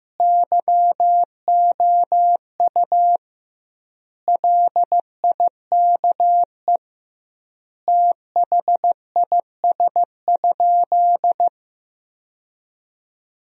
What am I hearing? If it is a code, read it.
Morse code, heard as a keyed tone, about 16 wpm, 700 Hz. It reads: YOU LIKE THIS?
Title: 2x Word-Spacing - Sentences from Top 200 Words - 15wpm